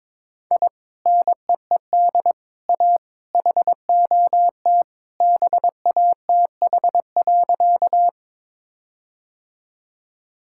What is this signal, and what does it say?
Morse code, heard as a keyed tone, about 22 wpm, 700 Hz. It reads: I NEED A HOT BATH.
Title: Sentences from Top 1600 Words 22wpm